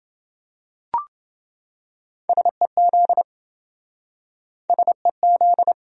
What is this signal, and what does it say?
Morse code, heard as a keyed tone, about 26 wpm, 700 Hz. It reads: SEZ SEZ